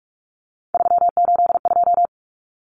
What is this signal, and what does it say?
Morse code, heard as a keyed tone, about 44 wpm, 700 Hz. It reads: 382
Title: Common Cut Numbers - 3-Digit 45wpm